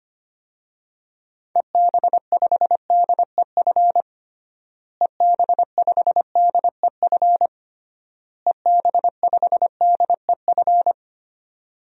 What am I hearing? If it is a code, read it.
Morse code, heard as a keyed tone, about 25 wpm, 700 Hz. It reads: EB5DEF EB5DEF EB5DEF
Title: DXCC-100 Call Signs - 6 Characters - Repeated 3x - 25wpm